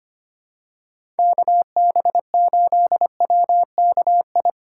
Morse code, heard as a keyed tone, about 25 wpm, 700 Hz. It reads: KB8WKI
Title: USA Call Signs - 6 Characters - 4x Character Spacing - 25wpm